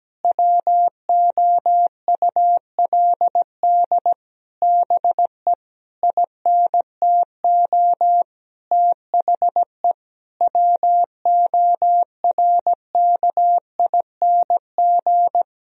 Morse code, heard as a keyed tone, about 17 wpm, 700 Hz. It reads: WOULD BE INTO THE WORKING
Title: Sentences from Top 1000 Words - Review Part 4 - 17wpm